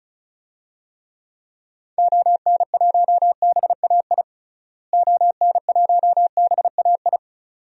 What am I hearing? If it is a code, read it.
Morse code, heard as a keyed tone, about 36 wpm, 700 Hz. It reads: ON1BAI ON1BAI
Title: DXCC-100 Call Signs - 6 Characters - Repeated 2x - 35wpm